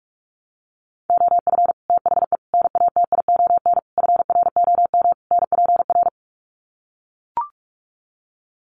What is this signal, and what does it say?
Morse code, heard as a keyed tone, about 44 wpm, 700 Hz. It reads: OF THE NATION FROM NPR
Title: Sets of 5 Words 45wpm